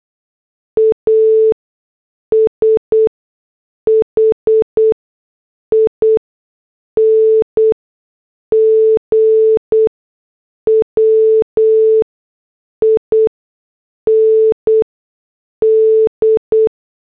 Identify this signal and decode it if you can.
Morse code, heard as a keyed tone, about 8 wpm, spaced as for 6 wpm, 440 Hz. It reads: ASHING WIND